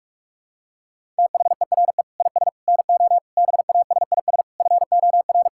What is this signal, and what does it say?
Morse code, heard as a keyed tone, about 46 wpm, 700 Hz. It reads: THERE IS NO BASIS FOR
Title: Sentences from Top 1000 Words - Review Part 5 - 45wpm